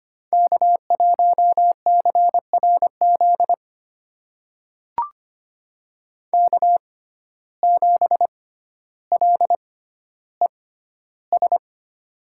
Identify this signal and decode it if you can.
Morse code, heard as a keyed tone, about 25 wpm, spaced as for 13 wpm, 700 Hz. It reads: K1CRZ K7LES
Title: USA Call Signs - 6x Character Spacing - 25wpm